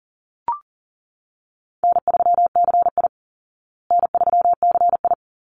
Morse code, heard as a keyed tone, about 40 wpm, 700 Hz. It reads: N3CI N3CI
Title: Call Signs Repeated 3x - 40wpm